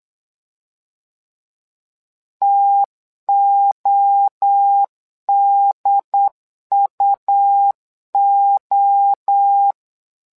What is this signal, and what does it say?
Morse code, heard as a keyed tone, about 8 wpm, 800 Hz. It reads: TODUO
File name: LEZIONI / 3 MWRO / 3 MWRO 8wpm 2606